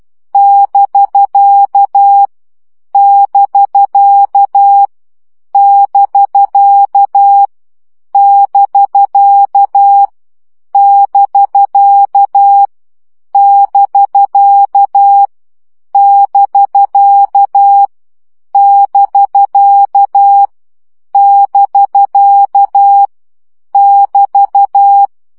Morse code, heard as a keyed tone, about 12 wpm, 800 Hz. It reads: <BK> <BK> <BK> <BK> <BK> <BK> <BK> <BK> <BK> <BT>
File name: CW gyakorlas bk 10 baud